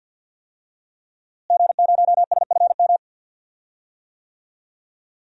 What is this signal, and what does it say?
Morse code, heard as a keyed tone, about 47 wpm, 700 Hz. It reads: G0SFM
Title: DXCC-100 Call Signs - 5 Characters - Repeated 2x - 50wpm